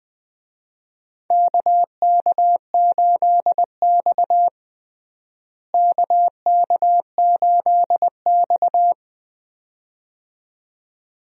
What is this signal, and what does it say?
Morse code, heard as a keyed tone, about 20 wpm, 700 Hz. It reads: KK8X KK8X
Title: USA Call Signs - 4 Characters - Repeated 2x - 20wpm